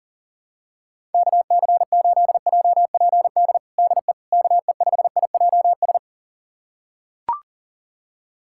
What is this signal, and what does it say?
Morse code, heard as a keyed tone, about 40 wpm, 700 Hz. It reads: KC8JPD DE KE5IJS
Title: USA Callsign DE Callsign - 6 Characters - 40wpm